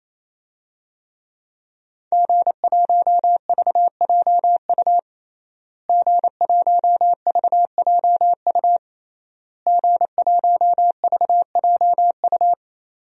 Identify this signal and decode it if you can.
Morse code, heard as a keyed tone, about 27 wpm, 700 Hz. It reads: G1VJU G1VJU G1VJU
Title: DXCC-100 Call Signs - 5 Characters - Repeated 3x - 28wpm